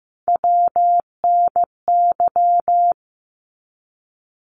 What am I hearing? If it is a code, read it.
Morse code, heard as a keyed tone, about 15 wpm, 700 Hz. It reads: WNY
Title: ARRL Sections 15wpm